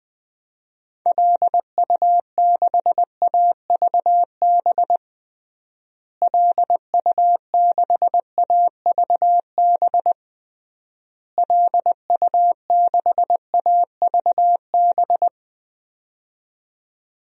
Morse code, heard as a keyed tone, about 21 wpm, 700 Hz. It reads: LU6AVB LU6AVB LU6AVB
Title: DXCC-100 Call Signs - 6 Characters - Repeated 3x - 20wpm